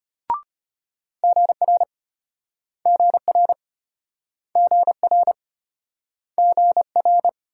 Morse code, heard as a keyed tone, about 34 wpm, 700 Hz. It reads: GR GR GR GR